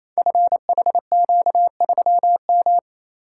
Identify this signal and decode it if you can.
Morse code, heard as a keyed tone, about 28 wpm, 700 Hz. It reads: FHQ3M
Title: Code Groups - 5 Letter:Numbers - 28wpm